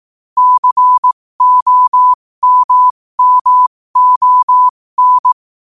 Morse code, heard as a keyed tone, about 18 wpm, spaced as for 15 wpm, 1000 Hz. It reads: COMMON
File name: qotdfiles / 15wpm